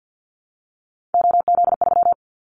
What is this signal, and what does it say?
Morse code, heard as a keyed tone, about 50 wpm, 700 Hz. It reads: OZ3
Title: Callsign Prefixes for Top 100 DXCC - 50wpm